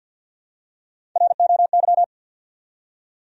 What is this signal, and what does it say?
Morse code, heard as a keyed tone, about 52 wpm, 700 Hz. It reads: ROY
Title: USA Common Names - Max 4 Letters - 50wpm